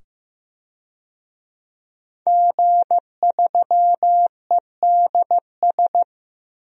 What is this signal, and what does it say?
Morse code, heard as a keyed tone, about 15 wpm, 700 Hz. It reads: G3EDS
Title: DXCC-100 Call Signs - 5 Characters - 15wpm